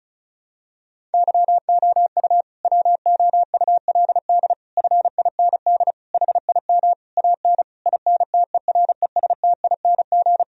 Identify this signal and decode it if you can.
Morse code, heard as a keyed tone, about 36 wpm, 700 Hz. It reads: YOU WOULD FIND HIM AN INTERESTING
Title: Sentences from Top 1000 Words - Review Part 1 - 35wpm